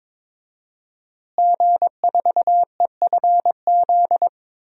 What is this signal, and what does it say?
Morse code, heard as a keyed tone, about 22 wpm, 700 Hz. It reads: G4EFZ
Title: DXCC-100 Call Signs - 5 Characters - Repeated 2x - 22wpm